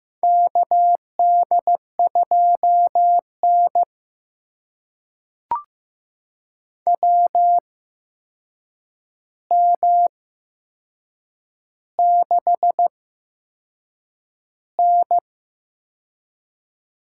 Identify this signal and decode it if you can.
Morse code, heard as a keyed tone, about 15 wpm, spaced as for 6 wpm, 700 Hz. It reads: KD2N WM6N
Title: USA Call Signs - 4 Characters - 8x Character Spacing - 15wpm